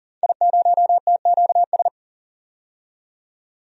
Morse code, heard as a keyed tone, about 40 wpm, 700 Hz. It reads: I0TQS
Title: DXCC-100 Call Signs - 5 Characters - 40wpm